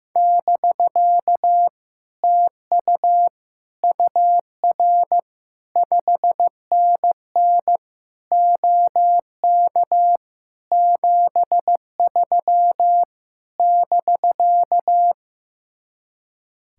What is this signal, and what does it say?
Morse code, heard as a keyed tone, about 16 wpm, 700 Hz. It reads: <BK> TU UR 5NN OK 73 <BK>
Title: Speed-Racing Common QSO Phrases Encore - 15wpm